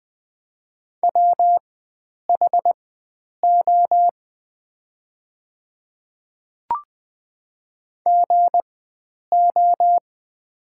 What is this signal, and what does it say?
Morse code, heard as a keyed tone, about 20 wpm, spaced as for 9 wpm, 690 Hz. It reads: WHO GO
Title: 4x Character Spacing - Top 100 Words - 20wpm